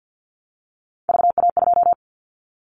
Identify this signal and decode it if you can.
Morse code, heard as a keyed tone, about 50 wpm, 700 Hz. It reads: VA2
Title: Callsign Prefixes for Top 25 DXCC - Repeated 3x - 50wpm